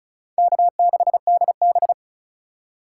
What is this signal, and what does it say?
Morse code, heard as a keyed tone, about 36 wpm, 700 Hz. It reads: K6DB
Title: USA Call Signs - 4 Characters - 3x Character Spacing - 35wpm